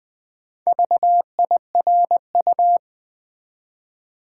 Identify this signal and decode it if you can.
Morse code, heard as a keyed tone, about 20 wpm, 700 Hz. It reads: VIRU
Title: Build A Word - 1 Letter At A Time - 3 to 5-Letter Words - 20wpm